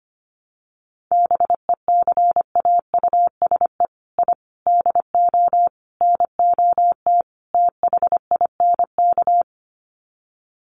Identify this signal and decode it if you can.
Morse code, heard as a keyed tone, about 25 wpm, 700 Hz. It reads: BECAUSE I DO NOT THINK
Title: Sets of 5 Words Encore 25wpm